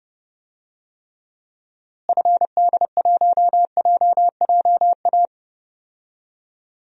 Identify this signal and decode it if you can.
Morse code, heard as a keyed tone, about 31 wpm, 700 Hz. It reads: FD1JJA
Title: DXCC-100 Call Signs - 6 Characters - Repeated 3x - 30wpm